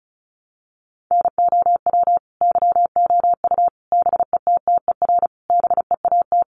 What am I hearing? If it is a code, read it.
Morse code, heard as a keyed tone, about 34 wpm, 700 Hz. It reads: NOW YOU BETTER BEAT
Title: Sentences from Top 1300 Words 35wpm